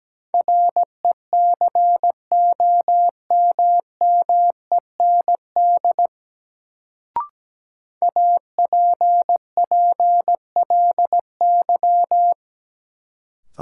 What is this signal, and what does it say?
Morse code, heard as a keyed tone, about 17 wpm, 700 Hz. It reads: RECOMMEND APPLY
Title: Sentences from Top 1200 Words - New Word Drill Alt - 17wpm